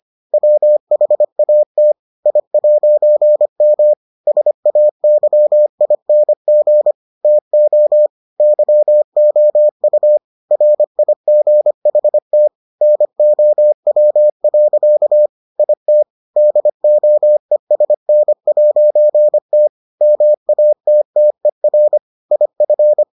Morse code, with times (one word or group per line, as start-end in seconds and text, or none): 0.3-1.9 WHAT
2.3-3.9 I'M
4.3-6.9 SAYING
7.2-8.1 TO
8.4-10.2 YOU
10.5-12.5 RIGHT
12.8-15.3 NOW.
15.6-16.0 IT
16.4-19.7 DOESN'T
20.0-22.0 MATTER
22.3-23.0 IF